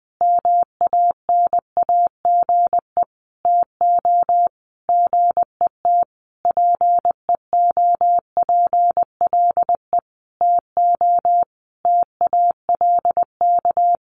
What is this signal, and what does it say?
Morse code, heard as a keyed tone, about 20 wpm, 700 Hz. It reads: MANAGE TO GET PEOPLE TO TALK
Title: Sentences from Top 1400 Words 20wpm